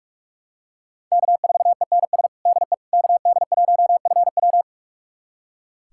Morse code, heard as a keyed tone, about 45 wpm, 700 Hz. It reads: K4ENS DE KD1FW
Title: USA Callsign DE Callsign 45wpm